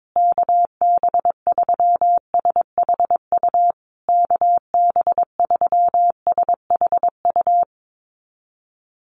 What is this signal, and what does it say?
Morse code, heard as a keyed tone, about 22 wpm, 700 Hz. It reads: KB3SHU KB3SHU
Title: USA Call Signs Repeated 3x 22wpm